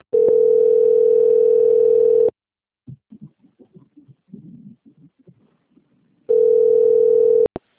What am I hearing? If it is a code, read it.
Ringback tone (North America), 440 + 480 Hz.